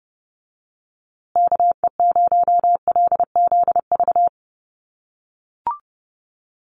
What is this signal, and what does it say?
Morse code, heard as a keyed tone, about 30 wpm, 700 Hz. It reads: KE0LZV
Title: USA Call Signs Repeated 2x 30wpm